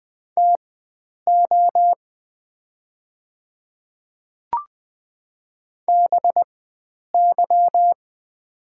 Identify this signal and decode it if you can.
Morse code, heard as a keyed tone, about 20 wpm, spaced as for 9 wpm, 700 Hz. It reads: TO BY
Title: 4x Character Spacing - Top 100 Words - 20wpm